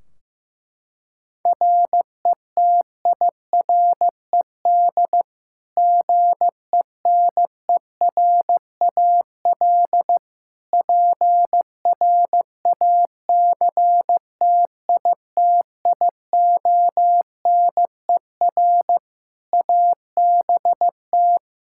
Morse code, heard as a keyed tone, about 15 wpm, 700 Hz. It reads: RETIRED GENERAL PRACTITIONER ABT
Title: Practice QSO - Exchange 3 - More Info - 15wpm